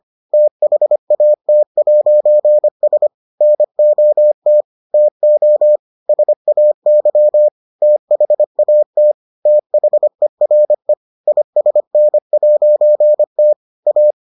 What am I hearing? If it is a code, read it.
Morse code, heard as a keyed tone, about 25 wpm, 600 Hz. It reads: THAT'S NOT TO SAY THAT THERE ISN'T A